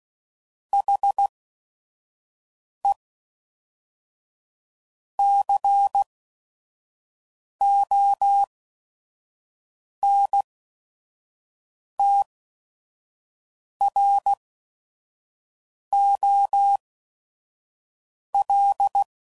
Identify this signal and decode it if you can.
Morse code, heard as a keyed tone, about 16 wpm, spaced as for 5 wpm, 800 Hz. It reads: HE CONTROL
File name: CW Code Groups @5WPM